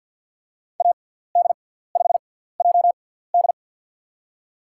Morse code, heard as a keyed tone, about 51 wpm, spaced as for 17 wpm, 700 Hz. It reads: AD5JD
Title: USA Call Signs - 5 Characters - 6x Character Spacing - 50wpm